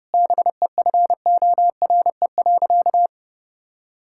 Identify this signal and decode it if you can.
Morse code, heard as a keyed tone, about 32 wpm, 700 Hz. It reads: BEFORE.
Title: Sentences from Top 1000 Words - Review Part 5 - 30wpm